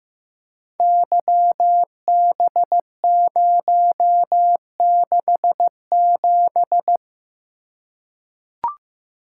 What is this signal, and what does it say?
Morse code, heard as a keyed tone, about 15 wpm, 700 Hz. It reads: YB067
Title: Code Groups - 5 Letter:Numbers - 15wpm